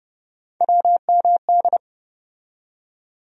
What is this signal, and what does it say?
Morse code, heard as a keyed tone, about 30 wpm, 700 Hz. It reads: WMD
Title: USA Call Sign Postfixes - 3 Letters 30wpm